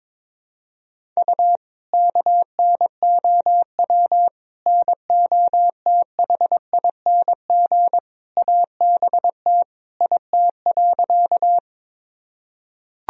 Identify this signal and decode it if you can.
Morse code, heard as a keyed tone, about 22 wpm, 700 Hz. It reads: U KNOW NOTHING ABT IT.
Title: Sentences from Top 500 Words and Abbreviations 22wpm